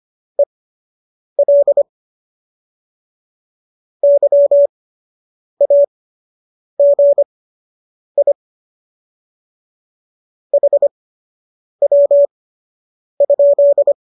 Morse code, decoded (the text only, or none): EL YAGI HW?